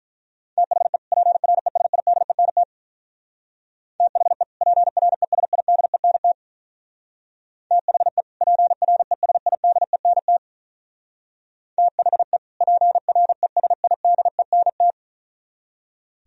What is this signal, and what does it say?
Morse code, heard as a keyed tone, about 42 wpm, 700 Hz. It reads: THE PRESIDENT THE PRESIDENT THE PRESIDENT THE PRESIDENT
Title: Speed-Racing - Sets of 2 Words 35wpm